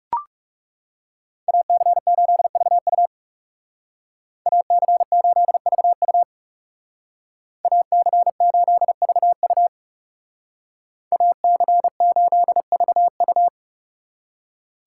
Morse code, heard as a keyed tone, about 37 wpm, 700 Hz. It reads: AC8VU AC8VU AC8VU AC8VU
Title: Speed Racing - USA 2x2 Call Signs 30wpm